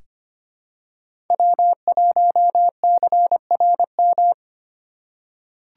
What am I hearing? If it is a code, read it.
Morse code, heard as a keyed tone, about 25 wpm, 700 Hz. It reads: W1CRM
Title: USA Call Signs - 4x Character Spacing - 25wpm